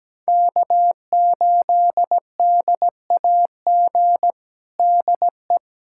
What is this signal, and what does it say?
Morse code, heard as a keyed tone, about 17 wpm, 700 Hz. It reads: K8DAG DE
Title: USA Callsign DE Callsign - 5 Characters - 17wpm